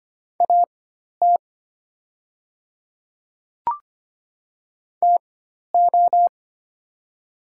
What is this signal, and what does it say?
Morse code, heard as a keyed tone, about 25 wpm, spaced as for 12 wpm, 700 Hz. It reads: AT TO
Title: 4x Character Spacing - Top 200 Words - 25wpm